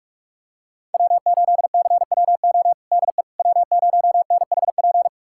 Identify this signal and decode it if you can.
Morse code, heard as a keyed tone, about 47 wpm, 700 Hz. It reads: W8CWO DE W0NHP
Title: USA Callsign DE Callsign - 5 Characters - 45wpm